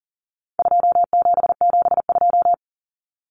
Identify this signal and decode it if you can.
Morse code, heard as a keyed tone, about 40 wpm, 700 Hz. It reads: 2772
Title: Common Cut Numbers - 4-Digit 40wpm